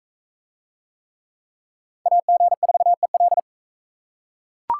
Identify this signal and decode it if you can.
Morse code, heard as a keyed tone, about 36 wpm, 700 Hz. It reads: AG4EL